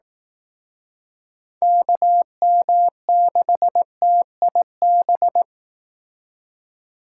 Morse code, heard as a keyed tone, about 18 wpm, 700 Hz. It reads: KM6TIB
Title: Speed Racing - USA 2x3 Call Signs 15wpm